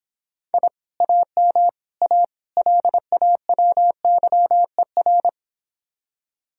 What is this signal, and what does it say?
Morse code, heard as a keyed tone, about 26 wpm, 700 Hz. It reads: I AM A LAWYER